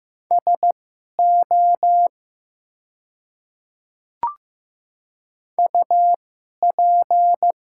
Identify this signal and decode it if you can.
Morse code, heard as a keyed tone, about 15 wpm, spaced as for 11 wpm, 700 Hz. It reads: SO UP
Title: 2x Character Spacing - Top 300 Words - 15wpm